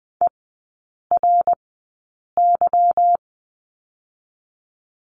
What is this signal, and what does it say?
Morse code, heard as a keyed tone, about 20 wpm, spaced as for 8 wpm, 700 Hz. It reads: ERY
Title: 145 Lesson 24e - Letter y - Three Characters 20wpm